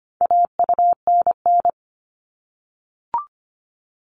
Morse code, heard as a keyed tone, about 25 wpm, 700 Hz. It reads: AUNN
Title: All Cut Numbers - 4-Digit 25wpm